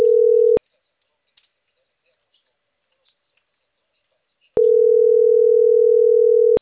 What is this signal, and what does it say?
Ringback tone (North America), 440 + 480 Hz.